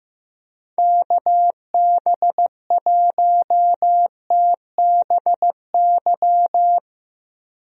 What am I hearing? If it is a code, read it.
Morse code, heard as a keyed tone, about 15 wpm, 700 Hz. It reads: KB1TBY
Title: USA Call Signs - 6 Characters - Repeated 3x - 15wpm